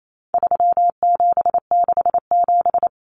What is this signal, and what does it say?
Morse code, heard as a keyed tone, about 28 wpm, 700 Hz. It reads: 3767
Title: Numbers - 4-Digit 28wpm